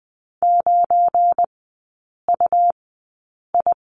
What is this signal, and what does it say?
Morse code, heard as a keyed tone, about 20 wpm, spaced as for 8 wpm, 700 Hz. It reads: 9UI
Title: 184 Lesson 30e - Letter b - Three Characters 20wpm